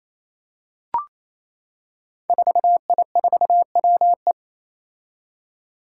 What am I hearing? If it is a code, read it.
Morse code, heard as a keyed tone, about 28 wpm, 700 Hz. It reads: 4I4WE
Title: Five Challenging Letter-Number Combinations 28wpm